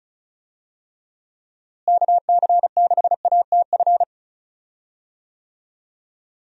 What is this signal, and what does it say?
Morse code, heard as a keyed tone, about 36 wpm, 700 Hz. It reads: KC6ATF
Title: USA Call Signs - 6 Characters - Repeated 3x - 35wpm